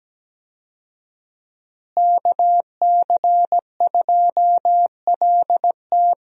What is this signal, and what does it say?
Morse code, heard as a keyed tone, about 17 wpm, 700 Hz. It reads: KC2LT
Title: USA Call Signs - 5 Characters - Repeated 2x - 17wpm